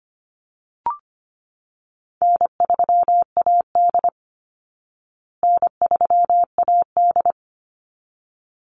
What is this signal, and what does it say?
Morse code, heard as a keyed tone, about 25 wpm, 700 Hz. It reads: N3AD N3AD
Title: Call Signs Repeated 3x - 25wpm